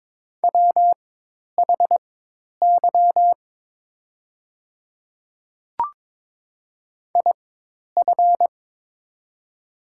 Morse code, heard as a keyed tone, about 22 wpm, spaced as for 10 wpm, 700 Hz. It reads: WHY IF
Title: 4x Character Spacing - Top 200 Words - 22wpm